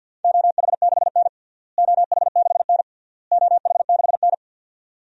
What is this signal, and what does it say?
Morse code, heard as a keyed tone, about 53 wpm, 700 Hz. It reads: OH6N OH6N OH6N
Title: DXCC-100 Call Signs - 4 Characters - Repeated 3x - 50wpm